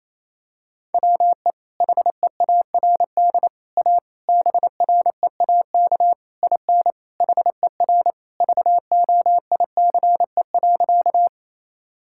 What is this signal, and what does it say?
Morse code, heard as a keyed tone, about 28 wpm, 700 Hz. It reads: WE HEARD A BREAK IN HER VOICE.
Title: Sentences from Top 1000 Words - Review Part 6 - 28wpm